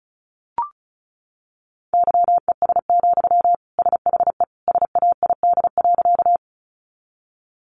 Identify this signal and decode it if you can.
Morse code, heard as a keyed tone, about 34 wpm, 700 Hz. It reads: YES, SHE SAID.
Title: Sentences from Top 500 Words 35wpm